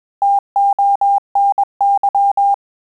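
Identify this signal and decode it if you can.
Morse code, heard as a keyed tone, about 21 wpm, 800 Hz. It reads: TONY
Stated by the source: Morse QSO practice at 20 words per minute, in Italian and English.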